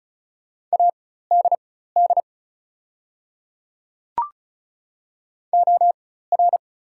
Morse code, heard as a keyed tone, about 34 wpm, spaced as for 16 wpm, 700 Hz. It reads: ADD OR
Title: 4x Character Spacing - Top 500 Words - 35wpm